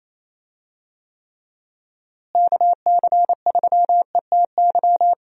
Morse code, heard as a keyed tone, about 28 wpm, 700 Hz. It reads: KC3ETY
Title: USA Call Signs - 6 Characters - Repeated 2x - 28wpm